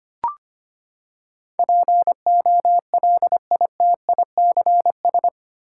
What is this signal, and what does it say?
Morse code, heard as a keyed tone, about 25 wpm, 700 Hz. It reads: POLITICS
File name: Eight Letter Words - 25wpm